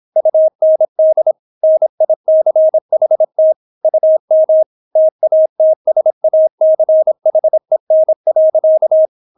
Morse code, heard as a keyed tone, about 26 wpm, 630 Hz. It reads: UND NICHT UM TATSACHEN.